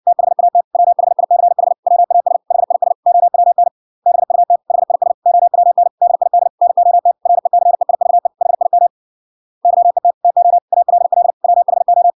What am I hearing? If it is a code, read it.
Morse code, heard as a keyed tone, about 56 wpm, 700 Hz. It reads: THAT WHICH WAS HIS OWN BUT HIS OWN DID NOT RECEIVE HIM YET TO ALL WHO